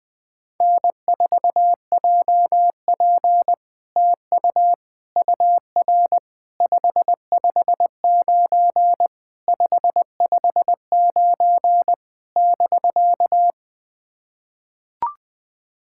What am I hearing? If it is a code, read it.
Morse code, heard as a keyed tone, about 20 wpm, 700 Hz. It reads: N4JP TU UR 559 559 <BK>
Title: SOTA QSO Practice 20wpm